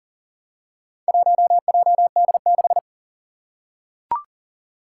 Morse code, heard as a keyed tone, about 39 wpm, 700 Hz. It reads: 1JD6
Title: Four Challenging Letter-Number Combinations 40wpm